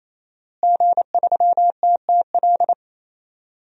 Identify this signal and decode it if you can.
Morse code, heard as a keyed tone, about 28 wpm, 700 Hz. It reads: G3TTL
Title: DXCC-100 Call Signs - 5 Characters - Repeated 3x - 28wpm